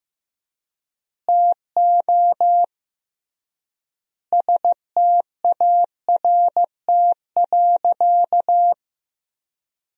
Morse code, heard as a keyed tone, about 15 wpm, 700 Hz. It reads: TO START.
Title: 3x Word-Spacing - Sentences from Top 300 Words - 15wpm